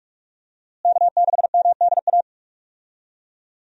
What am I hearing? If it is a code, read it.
Morse code, heard as a keyed tone, about 46 wpm, 700 Hz. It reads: K6MDA
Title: USA Call Signs - 5 Characters - Repeated 3x - 45wpm